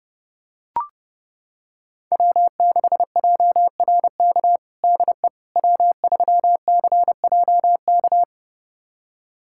Morse code, heard as a keyed tone, about 30 wpm, 700 Hz. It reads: W6JRK DE W3CJK
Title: USA Callsign DE Callsign - 5 Characters - 30wpm